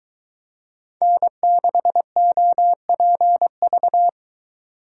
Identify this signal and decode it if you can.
Morse code, heard as a keyed tone, about 23 wpm, 700 Hz. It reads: N6OPV